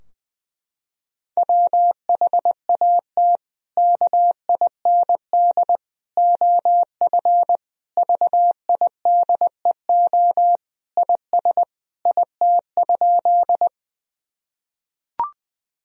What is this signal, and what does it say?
Morse code, heard as a keyed tone, about 21 wpm, 700 Hz. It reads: WHAT KIND OF VIDEO IS IT?
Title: Sentences from Top 1000 Words - Review Part 4 - 20wpm